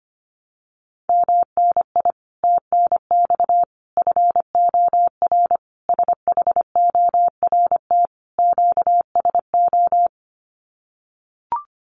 Morse code, heard as a keyed tone, about 25 wpm, 700 Hz. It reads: MNI TNX FOR SHORT QSO
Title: Common QSO Phrases 25wpm